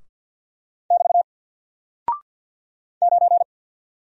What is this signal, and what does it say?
Morse code, heard as a keyed tone, about 48 wpm, 700 Hz. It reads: <BT> 9
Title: Single Character (Letter:Number:Punctuation:Prosign) 50wpm